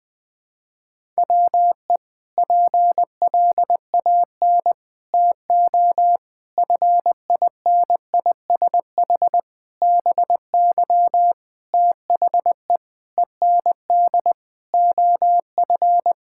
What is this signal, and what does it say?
Morse code, heard as a keyed tone, about 20 wpm, 700 Hz. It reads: WE PLAN TO FINISH BY THE END OF
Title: Sentences with 5-Word Phrases - 20wpm